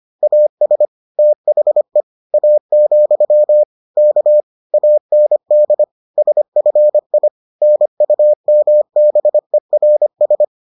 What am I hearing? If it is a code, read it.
Morse code, heard as a keyed tone, about 25 wpm, 600 Hz. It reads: AS THE A, K AND SFI NUMBERS